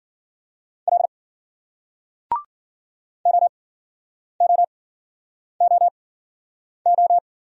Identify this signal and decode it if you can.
Morse code, heard as a keyed tone, about 54 wpm, spaced as for 9 wpm, 700 Hz. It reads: L OOOO